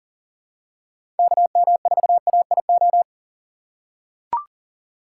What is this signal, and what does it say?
Morse code, heard as a keyed tone, about 40 wpm, 700 Hz. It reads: KM4AIO
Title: USA Call Signs - 6 Characters - 2x Character Spacing - 40wpm